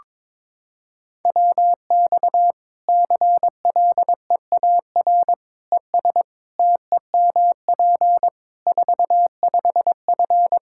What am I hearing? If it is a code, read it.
Morse code, heard as a keyed tone, about 22 wpm, 700 Hz. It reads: WX CLEAR ES TEMP 45F
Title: Practice QSO - Exchange 2 - Info - 22wpm